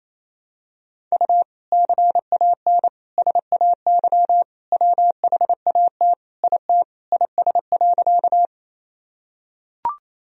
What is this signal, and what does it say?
Morse code, heard as a keyed tone, about 27 wpm, 700 Hz. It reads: U CAN SAY WHAT IT IS.
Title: Sentences from Top 200 Words and Abbreviations 28wpm